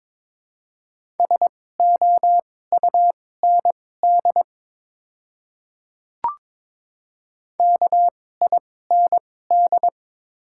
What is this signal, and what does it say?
Morse code, heard as a keyed tone, about 23 wpm, spaced as for 16 wpm, 700 Hz. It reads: SOUND KIND
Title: 2x Character Spacing - Top 200 Words - 22wpm